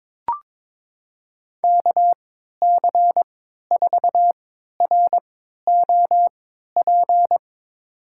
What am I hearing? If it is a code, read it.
Morse code, heard as a keyed tone, about 22 wpm, spaced as for 13 wpm, 700 Hz. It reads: KC4ROP
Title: USA Call Signs - 6 Characters - 3x Character Spacing - 22wpm